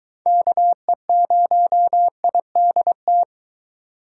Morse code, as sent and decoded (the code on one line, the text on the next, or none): -.- . ----- .. -.. -
KE0IDT